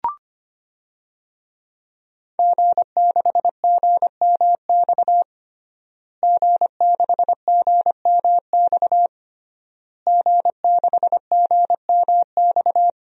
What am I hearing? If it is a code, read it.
Morse code, heard as a keyed tone, about 25 wpm, 700 Hz. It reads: G6GMX G6GMX G6GMX
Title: DXCC-100 Call Signs - 5 Characters - Repeated 3x - 25wpm